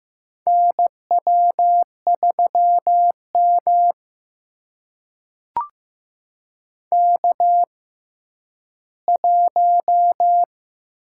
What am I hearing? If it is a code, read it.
Morse code, heard as a keyed tone, about 15 wpm, spaced as for 10 wpm, 700 Hz. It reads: NW3M K1
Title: USA Call Signs - 4 Characters - 6x Character Spacing - 15wpm